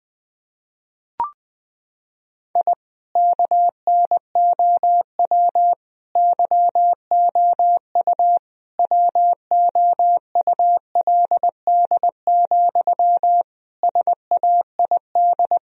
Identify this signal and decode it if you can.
Morse code, heard as a keyed tone, about 21 wpm, 700 Hz. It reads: I KNOW YOU WOULD, SAID
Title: Sentences from Top 1000 Words - Review Part 1 - 20wpm